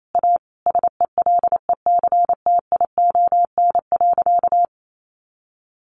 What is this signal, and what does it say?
Morse code, heard as a keyed tone, about 28 wpm, 700 Hz. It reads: A SELECTION.
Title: Sentences from Top 1400 Words 28wpm